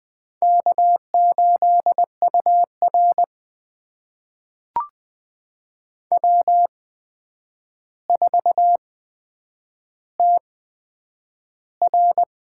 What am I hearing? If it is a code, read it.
Morse code, heard as a keyed tone, about 20 wpm, spaced as for 9 wpm, 700 Hz. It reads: K8UR W4TR
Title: USA Call Signs - 8x Character Spacing - 20wpm